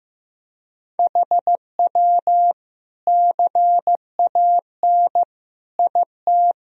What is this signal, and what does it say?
Morse code, heard as a keyed tone, about 15 wpm, 700 Hz. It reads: HW CAN IT
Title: Sentences from Top 100 Words and Abbreviations 15wpm